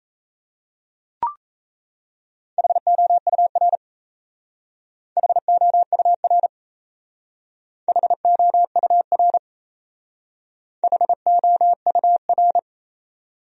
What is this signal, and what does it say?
Morse code, heard as a keyed tone, about 34 wpm, 690 Hz. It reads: HOUR HOUR HOUR HOUR
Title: Speed-Racing Top 500 Words 28wpm